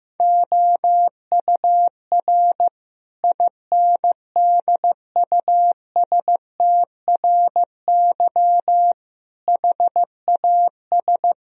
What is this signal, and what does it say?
Morse code, heard as a keyed tone, about 15 wpm, 700 Hz. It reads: OUR INDUSTRY HAS
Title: Sentences from Top 1000 Words - Review Part 4 - 15wpm